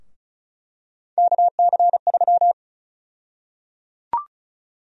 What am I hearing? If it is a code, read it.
Morse code, heard as a keyed tone, about 36 wpm, 700 Hz. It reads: KC3
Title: USA Call Sign Prefixes - Repeated 2x - 35wpm